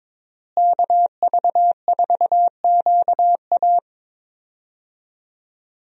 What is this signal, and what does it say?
Morse code, heard as a keyed tone, about 22 wpm, 700 Hz. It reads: KV4QA
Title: USA Call Signs - 5 Characters - Repeated 3x - 22wpm